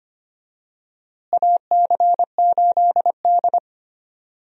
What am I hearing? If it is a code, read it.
Morse code, heard as a keyed tone, about 25 wpm, 700 Hz. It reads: AC8D
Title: USA Call Signs - 4 Characters - Repeated 3x - 25wpm